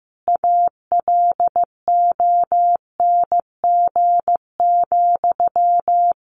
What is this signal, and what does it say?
Morse code, heard as a keyed tone, about 15 wpm, 700 Hz. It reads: ALONG,
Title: Sentences from Top 1500 Words 15wpm